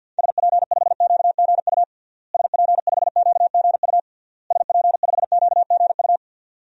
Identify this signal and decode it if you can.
Morse code, heard as a keyed tone, about 49 wpm, 700 Hz. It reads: SP5QGU SP5QGU SP5QGU
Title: DXCC-100 Call Signs - 6 Characters - Repeated 3x - 50wpm